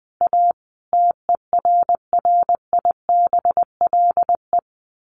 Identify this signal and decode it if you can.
Morse code, heard as a keyed tone, about 20 wpm, 700 Hz. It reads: A TERRIBLE
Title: A Sound Of Thunder 20wpm